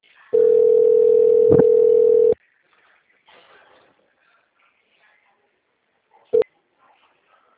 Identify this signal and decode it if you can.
Ringback tone (North America), 440 + 480 Hz.